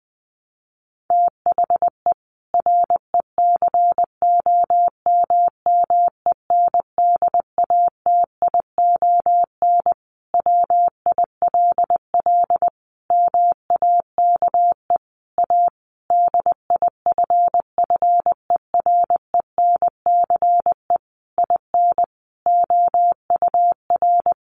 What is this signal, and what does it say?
Morse code, heard as a keyed tone, about 20 wpm, 700 Hz. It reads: THE RECOMMENDATION WILL MAKE A DIFFERENCE IN OUR
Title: Sentences from Top 1400 Words 20wpm